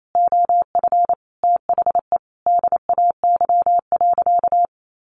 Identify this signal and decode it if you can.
Morse code, heard as a keyed tone, about 28 wpm, 700 Hz. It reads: OF THE DAY.
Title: Sentences from Top 1400 Words 28wpm